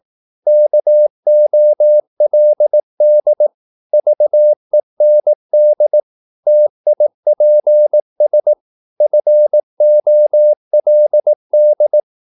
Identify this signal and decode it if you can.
Morse code, heard as a keyed tone, about 18 wpm, 600 Hz. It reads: KOLD VEND TIPS FOLD
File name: Korte ord DK 18wpm